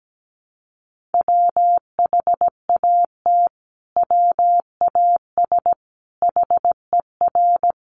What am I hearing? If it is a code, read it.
Morse code, heard as a keyed tone, about 17 wpm, 700 Hz. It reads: WHAT WAS HER
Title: Sentences from Top 1400 Words 17wpm